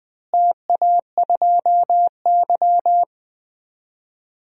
Morse code, heard as a keyed tone, about 21 wpm, 700 Hz. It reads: TA2Y
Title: DXCC-100 Call Signs - 4 Characters - Repeated 2x - 20wpm